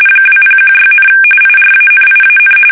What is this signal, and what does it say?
RTTY, about 45.45 baud. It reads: 2016, IRANIAN
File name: fsk850 / usnavy5